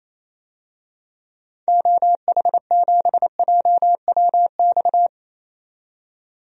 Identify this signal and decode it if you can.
Morse code, heard as a keyed tone, about 27 wpm, 700 Hz. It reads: OH7JWX
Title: DXCC-100 Call Signs - 6 Characters - 28wpm